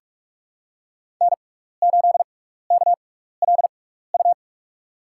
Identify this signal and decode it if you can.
Morse code, heard as a keyed tone, about 44 wpm, 700 Hz. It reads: N 8 K L U
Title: USA Call Signs - 5 Characters - 6x Character Spacing - 45wpm